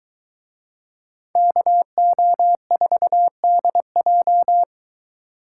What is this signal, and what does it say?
Morse code, heard as a keyed tone, about 21 wpm, 700 Hz. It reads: KO4DJ